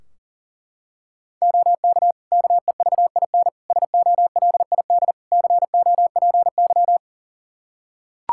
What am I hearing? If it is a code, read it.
Morse code, heard as a keyed tone, about 40 wpm, 700 Hz. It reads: OK KEVIN SOLID COPY
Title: Practice QSO - Exchange 3 - More Info - 40wpm